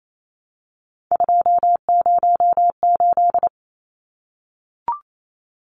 Morse code, heard as a keyed tone, about 28 wpm, 700 Hz. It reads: 208
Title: Numbers - 3-Digit 28wpm